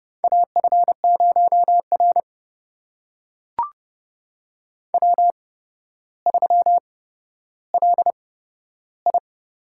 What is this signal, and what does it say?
Morse code, heard as a keyed tone, about 30 wpm, spaced as for 13 wpm, 700 Hz. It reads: AF0R W3LI
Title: USA Call Signs - 8x Character Spacing - 30wpm